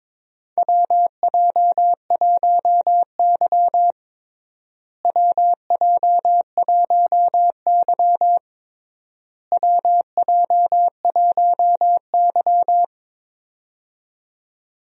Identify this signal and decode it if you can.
Morse code, heard as a keyed tone, about 21 wpm, 700 Hz. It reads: WJ1Y WJ1Y WJ1Y
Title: USA Call Signs - 4 Characters - Repeated 3x - 22wpm